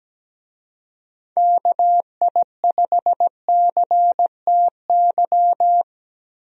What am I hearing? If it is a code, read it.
Morse code, heard as a keyed tone, about 17 wpm, 700 Hz. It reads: KI5CTY
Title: USA Call Signs - 6 Characters - Repeated 3x - 17wpm